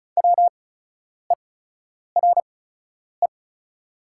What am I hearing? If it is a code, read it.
Morse code, heard as a keyed tone, about 34 wpm, spaced as for 10 wpm, 690 Hz. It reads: WERE
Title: 8x Character Spacing - Top 200 Words - 35wpm